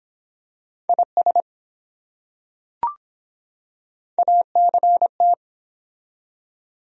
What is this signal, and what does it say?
Morse code, heard as a keyed tone, about 23 wpm, 700 Hz. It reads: IS ACT